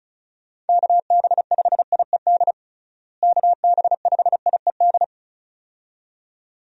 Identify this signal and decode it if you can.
Morse code, heard as a keyed tone, about 36 wpm, 700 Hz. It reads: KB5IED KB5IED
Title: USA Call Signs - 6 Characters - Repeated 2x - 35wpm